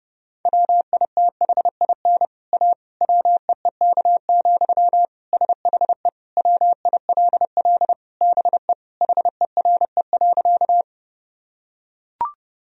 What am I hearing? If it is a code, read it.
Morse code, heard as a keyed tone, about 30 wpm, 700 Hz. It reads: WITHIN A WEEK, SHE WILL BE HERE.
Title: Sentences from Top 1000 Words - Review Part 4 - 30wpm